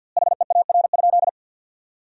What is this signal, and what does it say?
Morse code, heard as a keyed tone, about 48 wpm, 700 Hz. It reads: HEAR?
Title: 2x Word-Spacing - Sentences from Top 300 Words - 50wpm